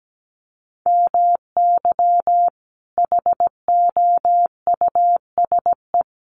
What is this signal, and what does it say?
Morse code, heard as a keyed tone, about 17 wpm, 700 Hz. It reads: MY HOUSE
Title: Sentences from Top 1100 Words 17wpm